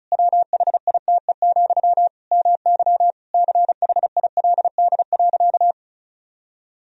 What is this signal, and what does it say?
Morse code, heard as a keyed tone, about 35 wpm, 700 Hz. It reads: WHITE, MY CHILD.
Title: Sentences from Top 1000 Words - Review Part 1 - 35wpm